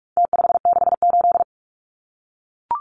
Morse code, heard as a keyed tone, about 44 wpm, 700 Hz. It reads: T568
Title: Numbers - Contest Style - 4-Digit 45wpm